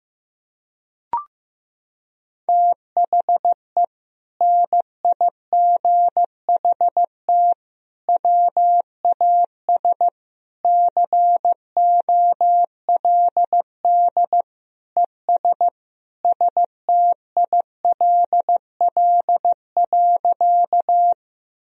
Morse code, heard as a keyed tone, about 15 wpm, 700 Hz. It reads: THE NIGHT WAS COLD ES STILL.
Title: Sentences from Top 400 Words and Abbreviations 15wpm